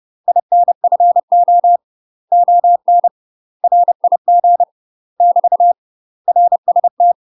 Morse code, heard as a keyed tone, about 30 wpm, 700 Hz. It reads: INFO ON RIG <BT> RST
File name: UK MASTER QSOs TRAINING / 30WPM / UK MASTER QSOs 0045